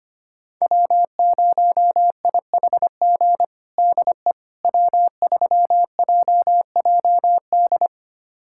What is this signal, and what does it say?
Morse code, heard as a keyed tone, about 25 wpm, 700 Hz. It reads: W0IHG DE W3JJD
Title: USA Callsign DE Callsign - 5 Characters - 25wpm